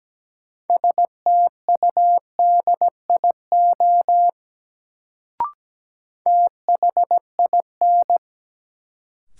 Morse code, heard as a keyed tone, about 17 wpm, 700 Hz. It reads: STUDIO THIN
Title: Sentences from Top 1200 Words - New Word Drill Alt - 17wpm